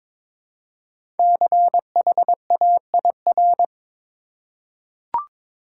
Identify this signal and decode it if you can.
Morse code, heard as a keyed tone, about 22 wpm, 700 Hz. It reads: CHAIR
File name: 5 Letter Words - First 3 Letters Given - 22wpm